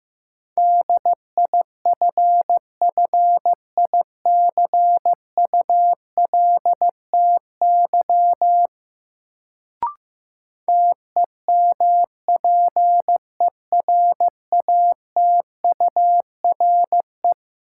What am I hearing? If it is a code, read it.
Morse code, heard as a keyed tone, about 15 wpm, 700 Hz. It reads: DIFFICULTY TEMPERATURE
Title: Sentences from Top 1500 Words - New Word Drill Alt - 15wpm